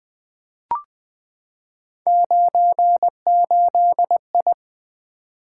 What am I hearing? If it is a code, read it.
Morse code, heard as a keyed tone, about 20 wpm, 700 Hz. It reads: 98I
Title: Three Letter-Number Combinations - 20wpm